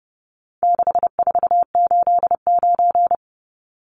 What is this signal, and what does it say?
Morse code, heard as a keyed tone, about 30 wpm, 700 Hz. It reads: 6489
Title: Numbers - 4-Digit 30wpm